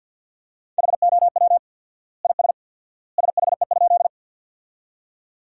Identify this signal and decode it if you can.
Morse code, heard as a keyed tone, about 46 wpm, 700 Hz. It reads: HOW IS SHE?
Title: 4x Word-Spacing - Sentences from Top 100 Words - 50wpm